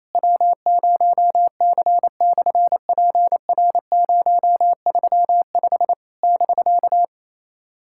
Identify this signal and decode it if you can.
Morse code, heard as a keyed tone, about 27 wpm, 700 Hz. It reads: W0C/PR035 <BK>
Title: SOTA QSO Practice 28wpm